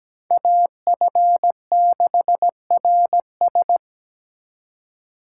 Morse code, heard as a keyed tone, about 20 wpm, 700 Hz. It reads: AF6RS
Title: Speed Racing - USA 2x2 Call Signs 17wpm